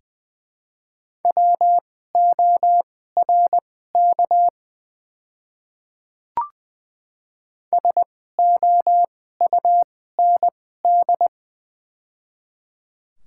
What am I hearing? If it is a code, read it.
Morse code, heard as a keyed tone, about 20 wpm, spaced as for 14 wpm, 700 Hz. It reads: WORK SOUND
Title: 2x Character Spacing - Top 200 Words - 20wpm